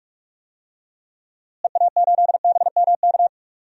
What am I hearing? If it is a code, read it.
Morse code, heard as a keyed tone, about 46 wpm, 700 Hz. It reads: EA8BMK
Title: DXCC-100 Call Signs - 6 Characters - Repeated 2x - 45wpm